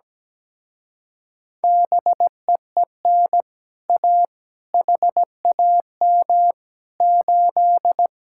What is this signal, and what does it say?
Morse code, heard as a keyed tone, about 17 wpm, 700 Hz. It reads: BEEN A HAM 8
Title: Speed-Racing Common QSO Phrases Encore - 17wpm